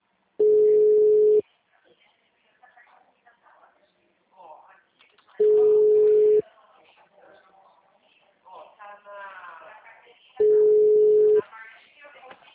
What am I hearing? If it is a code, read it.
Ringback tone (Europe, 425 Hz), with speech.
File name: out- -2005- - - .3581